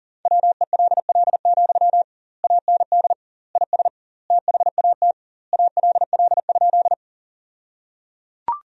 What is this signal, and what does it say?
Morse code, heard as a keyed tone, about 37 wpm, 700 Hz. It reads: WELL, AND IS THAT ALL?
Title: 2x Word-Spacing - Sentences from Top 200 Words - 40wpm